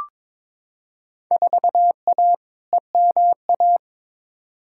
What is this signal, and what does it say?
Morse code, heard as a keyed tone, about 22 wpm, 700 Hz. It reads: 4A EMA
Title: Field Day Exchanges - 22wpm